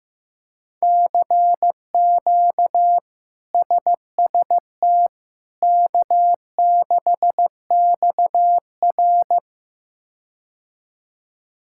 Morse code, heard as a keyed tone, about 15 wpm, 700 Hz. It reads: CQ SST K6XR
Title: K1USN SST QSO Practice - 15wpm